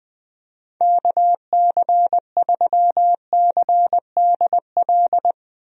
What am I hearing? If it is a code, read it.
Morse code, heard as a keyed tone, about 20 wpm, 700 Hz. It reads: KC3CDL
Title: USA Call Signs - 6 Characters - 20wpm